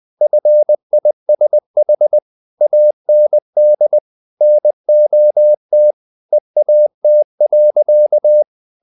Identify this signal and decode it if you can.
Morse code, heard as a keyed tone, about 20 wpm, 600 Hz. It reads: FISH AND NOT EAT.
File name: the old man and the sea / the old man and the sea part 1 20wpm